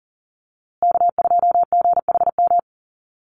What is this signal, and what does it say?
Morse code, heard as a keyed tone, about 40 wpm, 700 Hz. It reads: K2GHM
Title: USA Call Signs Repeated 3x 40wpm